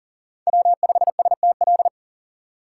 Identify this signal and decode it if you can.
Morse code, heard as a keyed tone, about 40 wpm, 700 Hz. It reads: W5STL
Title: USA Call Signs - 5 Characters - 2x Character Spacing - 40wpm